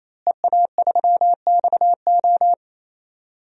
Morse code, heard as a keyed tone, about 28 wpm, 700 Hz. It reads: EA3XO
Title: DXCC-100 Call Signs - 5 Characters - Repeated 3x - 28wpm